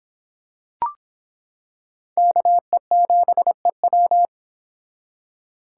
Morse code, heard as a keyed tone, about 26 wpm, 700 Hz. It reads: KE7EW